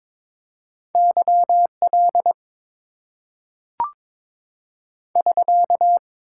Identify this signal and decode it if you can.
Morse code, heard as a keyed tone, about 22 wpm, 690 Hz. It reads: YL <SK>
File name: Top 90 QSO Elements - 8x Character Spacing - 22wpm